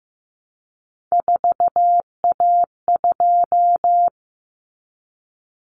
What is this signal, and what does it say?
Morse code, heard as a keyed tone, about 15 wpm, 700 Hz. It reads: 4A2
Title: Common Cut Numbers - 3-Digit 15wpm